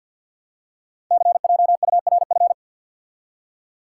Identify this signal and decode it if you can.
Morse code, heard as a keyed tone, about 50 wpm, 700 Hz. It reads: CJURF
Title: Code Groups - 5 Letters - 50wpm